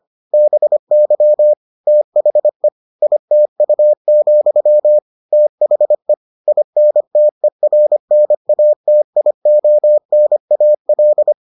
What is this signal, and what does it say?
Morse code, heard as a keyed tone, about 25 wpm, 600 Hz. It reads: BY THE ITU, THE INTERNATIONAL